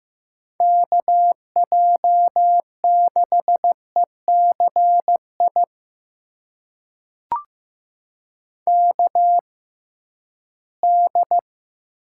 Morse code, heard as a keyed tone, about 15 wpm, spaced as for 11 wpm, 700 Hz. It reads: KJ6ECI KD
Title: USA Call Signs - 6 Characters - 6x Character Spacing - 15wpm